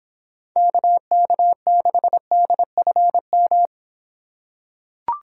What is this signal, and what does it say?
Morse code, heard as a keyed tone, about 24 wpm, 700 Hz. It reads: KK6DFM